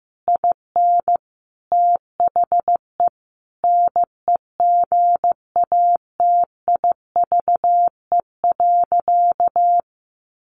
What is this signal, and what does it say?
Morse code, heard as a keyed tone, about 15 wpm, 700 Hz. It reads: IN THE NEGATIVE.
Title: Sentences from Top 1000 Words 15wpm